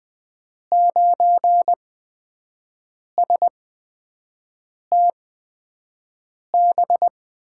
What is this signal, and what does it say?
Morse code, heard as a keyed tone, about 20 wpm, spaced as for 5 wpm, 700 Hz. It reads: 9STB
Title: USA Call Signs - 6 Characters - 8x Character Spacing - 20wpm